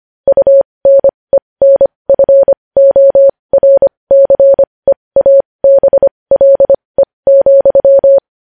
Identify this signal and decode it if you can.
Morse code, heard as a keyed tone, about 25 wpm, spaced as for 20 wpm, 550 Hz. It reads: UNENFORCEABLE,
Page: news headlines in morse code at 20 wpm